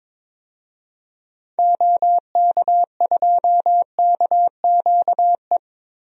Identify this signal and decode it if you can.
Morse code, heard as a keyed tone, about 21 wpm, 700 Hz. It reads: OK2KQE
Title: DXCC-100 Call Signs - 6 Characters - Repeated 2x - 22wpm